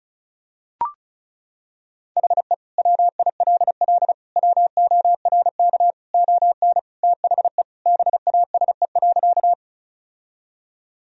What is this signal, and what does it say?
Morse code, heard as a keyed tone, about 36 wpm, 690 Hz. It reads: HE WILL WORK ON THE BASE.
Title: Sentences from Top 1000 Words - Review Part 1 - 35wpm